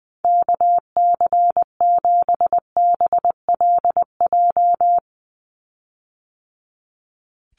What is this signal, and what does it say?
Morse code, heard as a keyed tone, about 20 wpm, 700 Hz. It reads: KC7BLJ
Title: USA Call Signs 20wpm